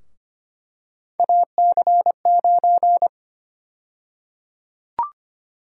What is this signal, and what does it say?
Morse code, heard as a keyed tone, about 25 wpm, 700 Hz. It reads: AC9
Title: USA Call Sign Prefixes - 25wpm